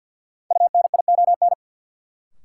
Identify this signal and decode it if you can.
Morse code, heard as a keyed tone, about 47 wpm, 700 Hz. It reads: UNION
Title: Sets of 5 Words Encore - New Word Drill - 50wpm